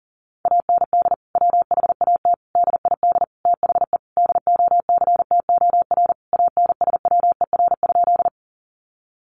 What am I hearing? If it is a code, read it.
Morse code, heard as a keyed tone, about 40 wpm, 700 Hz. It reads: AND WHAT DID THE DOCTOR ANSWER?
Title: Sentences from Top 1500 Words 40wpm